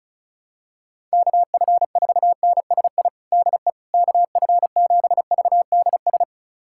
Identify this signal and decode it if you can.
Morse code, heard as a keyed tone, about 36 wpm, 700 Hz. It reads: KF4NSI DE KF7VDS
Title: USA Callsign DE Callsign - 6 Characters - 35wpm